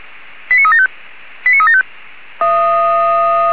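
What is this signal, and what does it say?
Five-tone selective call (ZVEI), heard as quick runs of five tones, tones of about 70 ms. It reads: 89375 89375
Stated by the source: ZVEI five-tone call 89374 received by SDR.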